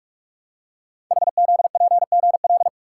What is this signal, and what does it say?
Morse code, heard as a keyed tone, about 44 wpm, 700 Hz. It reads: HZPGL
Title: Code Groups - 5 Letters - 45wpm